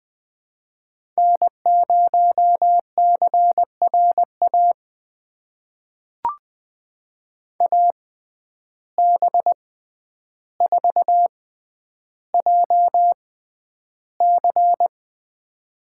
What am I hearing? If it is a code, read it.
Morse code, heard as a keyed tone, about 20 wpm, spaced as for 10 wpm, 700 Hz. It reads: N0CRA AB4JC
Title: USA Call Signs - 5 Characters - 6x Character Spacing - 20wpm